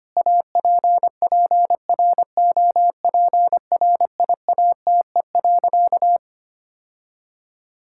Morse code, heard as a keyed tone, about 25 wpm, 700 Hz. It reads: APPROPRIATE.
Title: Sentences from Top 1000 Words - Review Part 5 - 25wpm